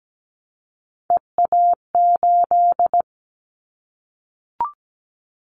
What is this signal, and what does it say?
Morse code, heard as a keyed tone, about 17 wpm, 700 Hz. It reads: EA8
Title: Callsign Prefixes for Top 100 DXCC - 17wpm